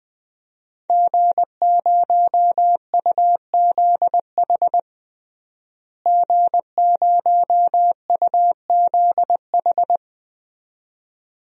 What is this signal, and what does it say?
Morse code, heard as a keyed tone, about 20 wpm, 700 Hz. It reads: G0UZH G0UZH
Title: DXCC-100 Call Signs - 5 Characters - Repeated 2x - 20wpm